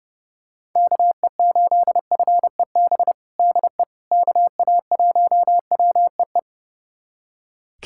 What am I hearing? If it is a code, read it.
Morse code, heard as a keyed tone, about 30 wpm, 700 Hz. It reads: KE8FEB DE KA1WEE
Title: USA Callsign DE Callsign - 6 Characters - 30wpm